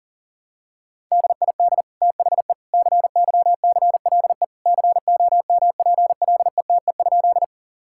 Morse code, heard as a keyed tone, about 40 wpm, 700 Hz. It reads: DID THE CYCLE COMPLETE?
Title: Sentences from Top 1000 Words - Review Part 6 - 40wpm